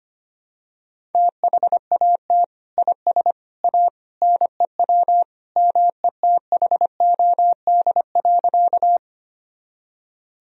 Morse code, heard as a keyed tone, about 25 wpm, 700 Hz. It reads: THAT IS A NEW METHOD.
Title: Sentences from Top 1000 Words - Review Part 3 - 25wpm